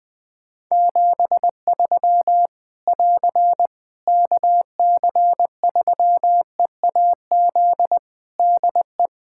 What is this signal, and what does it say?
Morse code, heard as a keyed tone, about 21 wpm, 700 Hz. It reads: 73 <AR> KC3EAZ DE
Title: Practice QSO - Exchange 4 - Ending - 20wpm